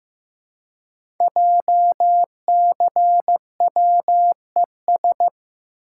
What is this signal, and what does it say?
Morse code, heard as a keyed tone, about 15 wpm, 700 Hz. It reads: JCWES
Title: Code Groups - 5 Letters - 15wpm